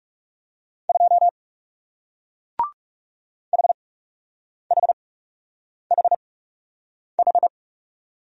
Morse code, heard as a keyed tone, about 41 wpm, spaced as for 8 wpm, 700 Hz. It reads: 2 HHHH